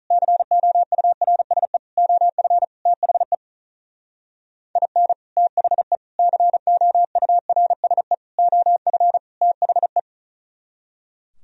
Morse code, heard as a keyed tone, about 45 wpm, 700 Hz. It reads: COURSE OF THE IN THE COURSE OF THE